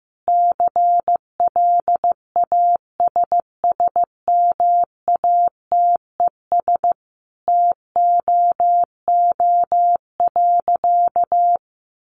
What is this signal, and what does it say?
Morse code, heard as a keyed tone, about 15 wpm, 700 Hz. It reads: CLASSMATES TOO.
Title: Additional Easy Phrases 15wpm